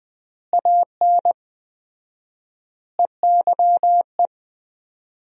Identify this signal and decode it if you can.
Morse code, heard as a keyed tone, about 20 wpm, 700 Hz. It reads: AN EYE
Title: 4x Word-Spacing - Sentences from Top 300 Words - 20wpm